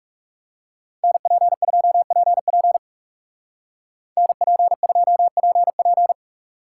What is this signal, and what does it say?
Morse code, heard as a keyed tone, about 36 wpm, 700 Hz. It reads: NP2PP NP2PP
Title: Speed Racing - USA 2x2 Call Signs 30wpm